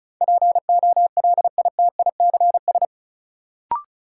Morse code, heard as a keyed tone, about 36 wpm, 700 Hz. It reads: POLITICS
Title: Sentences from Top 1000 Words - New Word Drill - 35wpm